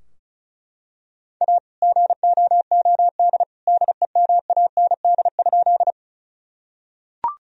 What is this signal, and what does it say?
Morse code, heard as a keyed tone, about 36 wpm, 700 Hz. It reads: A GOOD DEMAND?
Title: Sentences from Top 1000 Words - Review Part 4 - 35wpm